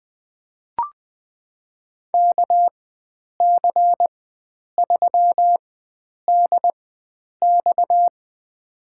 Morse code, heard as a keyed tone, about 20 wpm, spaced as for 9 wpm, 700 Hz. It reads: KC3DX
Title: USA Call Signs - 5 Characters - 4x Character Spacing - 20wpm